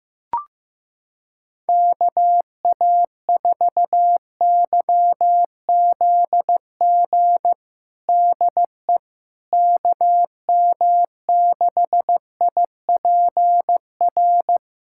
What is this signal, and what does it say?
Morse code, heard as a keyed tone, about 15 wpm, 700 Hz. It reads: KA4YZG DE KM6IPR
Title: USA Callsign DE Callsign - 6 Characters - 15wpm